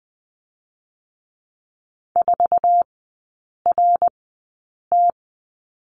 Morse code, heard as a keyed tone, about 20 wpm, spaced as for 8 wpm, 700 Hz. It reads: 4RT
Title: 060 Lesson 11e - Letter h - Three Characters 20wpm